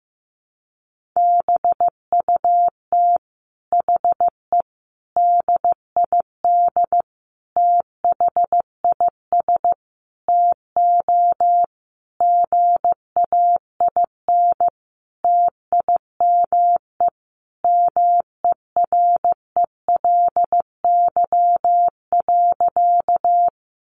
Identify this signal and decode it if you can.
Morse code, heard as a keyed tone, about 15 wpm, 700 Hz. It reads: BUT HE DID THIS TO GAIN TIME MERELY.
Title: Sentences from Top 1400 Words 15wpm